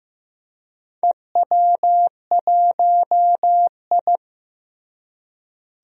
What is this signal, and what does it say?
Morse code, heard as a keyed tone, about 15 wpm, 700 Hz. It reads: EW1I
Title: DXCC-100 Call Signs - 4 Characters - Repeated 2x - 15wpm